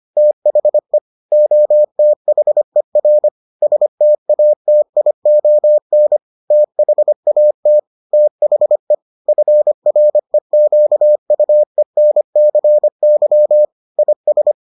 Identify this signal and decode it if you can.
Morse code, heard as a keyed tone, about 25 wpm, 600 Hz. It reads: THE OTHER STATION THAT THE FREQUENCY IS